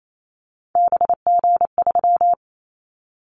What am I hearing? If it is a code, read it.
Morse code, heard as a keyed tone, about 28 wpm, 700 Hz. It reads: BG3
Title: Callsign Prefixes for Top 100 DXCC - 28wpm